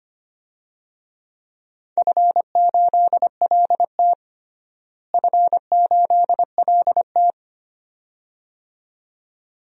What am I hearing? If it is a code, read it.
Morse code, heard as a keyed tone, about 25 wpm, 700 Hz. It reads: F8LT F8LT
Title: DXCC-100 Call Signs - 4 Characters - Repeated 2x - 25wpm